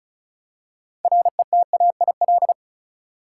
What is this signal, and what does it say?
Morse code, heard as a keyed tone, about 36 wpm, 700 Hz. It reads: RETAIL
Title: Practice QSO - Exchange 3 - More Info - 35wpm